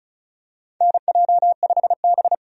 Morse code, heard as a keyed tone, about 36 wpm, 700 Hz. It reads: NJ5B
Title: USA Call Signs - 4 Characters - 35wpm